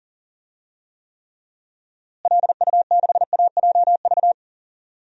Morse code, heard as a keyed tone, about 40 wpm, 700 Hz. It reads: LU6AJV
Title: DXCC-100 Call Signs - 6 Characters - Repeated 2x - 40wpm